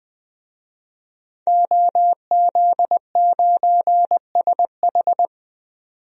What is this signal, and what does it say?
Morse code, heard as a keyed tone, about 20 wpm, 700 Hz. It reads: OZ9SH
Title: DXCC-100 Call Signs - 5 Characters - 20wpm